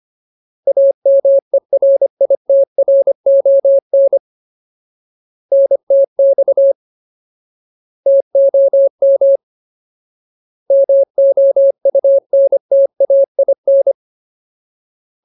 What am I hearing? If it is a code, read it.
Morse code, heard as a keyed tone, about 25 wpm, 550 Hz. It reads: AMERITRON NTX TOM MOUNTAIN
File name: Ultimate Word List / 25WPM Ultimate